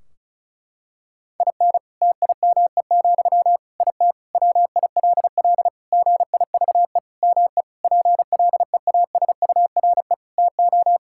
Morse code, heard as a keyed tone, about 34 wpm, 700 Hz. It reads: IN TIME, IT WILL GIVE ME PLEASURE TO
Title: Sentences from Top 1600 Words 35wpm